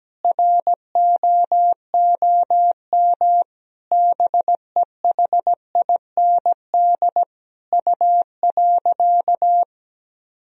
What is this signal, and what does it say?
Morse code, heard as a keyed tone, about 17 wpm, 700 Hz. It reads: ROOM BEHIND U.
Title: Sentences from Top 500 Words and Abbreviations 17wpm